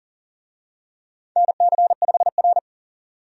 Morse code, heard as a keyed tone, about 40 wpm, 700 Hz. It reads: NC5R
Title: USA Call Signs - 4 Characters - 40wpm